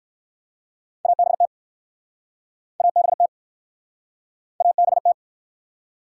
Morse code, heard as a keyed tone, about 61 wpm, 700 Hz. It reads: ABT ABT ABT